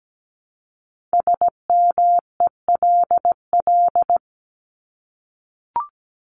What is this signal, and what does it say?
Morse code, heard as a keyed tone, about 17 wpm, 700 Hz. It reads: SMELL
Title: Top 1501-1750 Words 17wpm